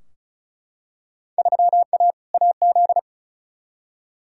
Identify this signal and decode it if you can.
Morse code, heard as a keyed tone, about 36 wpm, 700 Hz. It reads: 3A AZ
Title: Field Day Exchanges - 35wpm